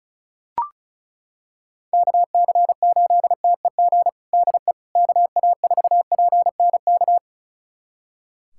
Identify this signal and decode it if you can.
Morse code, heard as a keyed tone, about 34 wpm, 700 Hz. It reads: KC8TEG DE KA4PNK
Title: USA Callsign DE Callsign 35wpm